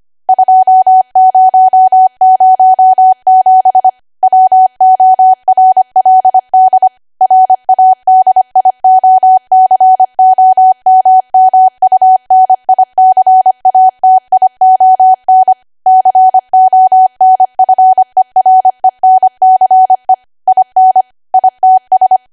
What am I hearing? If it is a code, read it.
Morse code, heard as a keyed tone, about 25 wpm, 750 Hz. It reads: 2007 WORLD RADIOCOMMUNICATION CONFERENCE IN ITS
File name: cwa / W1AW-Level3 / 25WPM